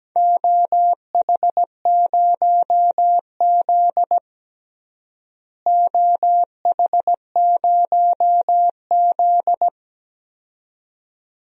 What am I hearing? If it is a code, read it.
Morse code, heard as a keyed tone, about 17 wpm, 700 Hz. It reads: OH0Z OH0Z
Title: DXCC-100 Call Signs - 4 Characters - Repeated 2x - 17wpm